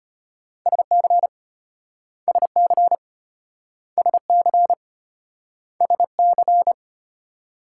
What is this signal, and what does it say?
Morse code, heard as a keyed tone, about 33 wpm, 700 Hz. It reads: SC SC SC SC